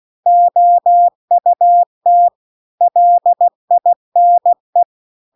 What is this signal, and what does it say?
Morse code, heard as a keyed tone, about 16 wpm, 700 Hz. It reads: OUT LINE
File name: CW-The-Big-Twoheaded-River-2-16WPM